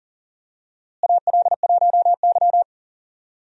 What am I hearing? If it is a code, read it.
Morse code, heard as a keyed tone, about 40 wpm, 700 Hz. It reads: AL1Y
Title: USA Call Signs - 4 Characters - Repeated 3x - 40wpm